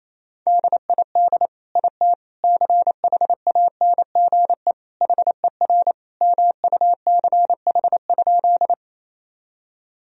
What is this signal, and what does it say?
Morse code, heard as a keyed tone, about 28 wpm, 700 Hz. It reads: DID IT CHANGE HER MUCH?
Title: Sentences from Top 1000 Words - Review Part 1 - 28wpm